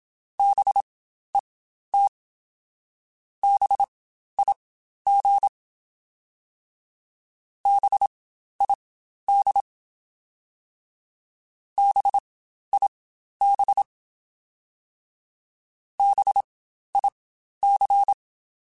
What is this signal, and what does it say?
Morse code, heard as a keyed tone, about 27 wpm, spaced as for 12 wpm, 800 Hz. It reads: BET BIG BID BIB BIC
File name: Spaced Alphas / Spaced Alphas mp3s / Bs sp @25